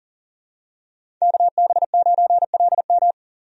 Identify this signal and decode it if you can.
Morse code, heard as a keyed tone, about 37 wpm, 700 Hz. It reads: KB9LM
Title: USA Call Signs - 5 Characters - 40wpm